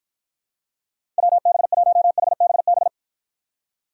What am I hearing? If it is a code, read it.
Morse code, heard as a keyed tone, about 53 wpm, 700 Hz. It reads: WB1HBB